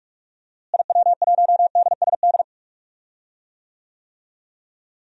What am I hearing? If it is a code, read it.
Morse code, heard as a keyed tone, about 41 wpm, 700 Hz. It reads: IW1DSD
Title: DXCC-100 Call Signs - 6 Characters - Repeated 2x - 45wpm